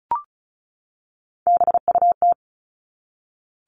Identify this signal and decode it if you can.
Morse code, heard as a keyed tone, about 34 wpm, 700 Hz. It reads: BUT
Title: Top 300 Words 35wpm